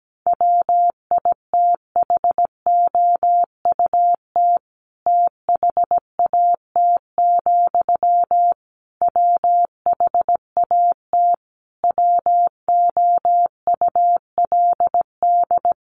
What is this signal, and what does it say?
Morse code, heard as a keyed tone, about 17 wpm, 700 Hz. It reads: WITHOUT THAT, WHAT WOULD
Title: Sentences from Top 600 Words 17wpm